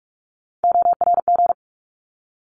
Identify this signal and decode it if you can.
Morse code, heard as a keyed tone, about 46 wpm, 700 Hz. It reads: ORG
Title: ARRL Sections 45wpm